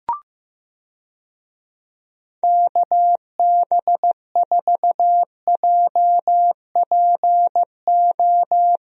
Morse code, heard as a keyed tone, about 15 wpm, 700 Hz. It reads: KB4JPO
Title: USA Call Signs - 6 Characters - Repeated 3x - 15wpm